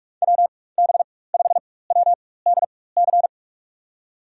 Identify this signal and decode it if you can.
Morse code, heard as a keyed tone, about 45 wpm, spaced as for 21 wpm, 700 Hz. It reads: WB5WDC
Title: USA Call Signs - 6 Characters - 4x Character Spacing - 45wpm